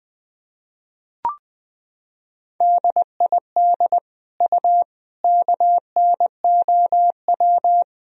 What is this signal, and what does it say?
Morse code, heard as a keyed tone, about 20 wpm, 700 Hz. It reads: DID U KNOW
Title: Sentences from Top 100 Words and Abbreviations 20wpm